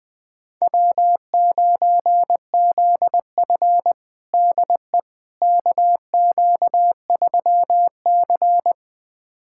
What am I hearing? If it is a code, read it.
Morse code, heard as a keyed tone, about 21 wpm, 700 Hz. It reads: W9ZF DE KQ3C
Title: USA Callsign DE Callsign - 4 Characters - 20wpm